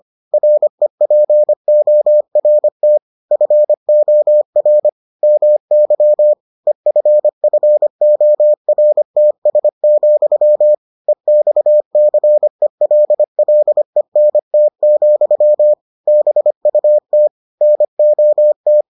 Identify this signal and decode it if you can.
Morse code, heard as a keyed tone, about 25 wpm, 600 Hz. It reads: REPORT FOR MY EFFORTS, EXCELLENT, BUT NOT